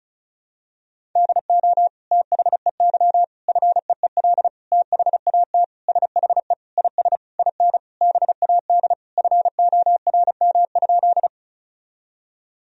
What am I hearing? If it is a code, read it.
Morse code, heard as a keyed tone, about 36 wpm, 700 Hz. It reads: DO THEY FEEL THAT SHE IS IN BAD FORM?
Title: Sentences from Top 1000 Words - Review Part 1 - 35wpm